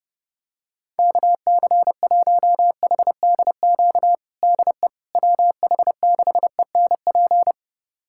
Morse code, heard as a keyed tone, about 30 wpm, 700 Hz. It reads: KC1HDQ DE WH6ENP
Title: USA Callsign DE Callsign - 6 Characters - 30wpm